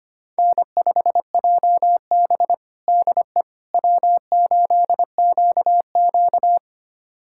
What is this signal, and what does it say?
Morse code, heard as a keyed tone, about 25 wpm, 700 Hz. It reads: N5JB DE W8QQ
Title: USA Callsign DE Callsign - 4 Characters - 25wpm